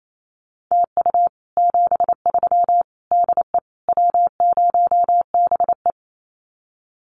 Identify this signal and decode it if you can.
Morse code, heard as a keyed tone, about 28 wpm, 700 Hz. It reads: TU 73 DE W0BE
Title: Common QSO Phrases Encore 28wpm